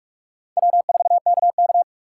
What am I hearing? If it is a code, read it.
Morse code, heard as a keyed tone, about 39 wpm, 700 Hz. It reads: W4KK